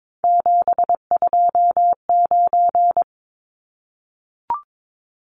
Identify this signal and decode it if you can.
Morse code, heard as a keyed tone, about 22 wpm, 700 Hz. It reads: 729
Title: Numbers - 3-Digit 22wpm